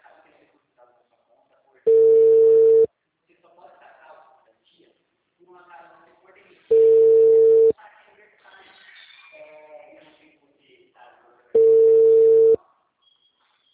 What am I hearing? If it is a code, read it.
Ringback tone (China, 450 Hz), with speech.